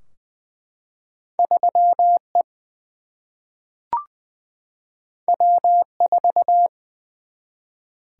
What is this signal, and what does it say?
Morse code, heard as a keyed tone, about 20 wpm, 700 Hz. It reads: 3 E W 4
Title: Single-Letter Single-Number 20wpm